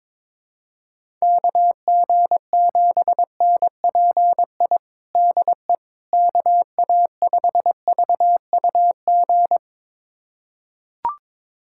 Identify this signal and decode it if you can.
Morse code, heard as a keyed tone, about 22 wpm, 700 Hz. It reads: KG7NPI DE KA5VUG
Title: USA Callsign DE Callsign - 6 Characters - 22wpm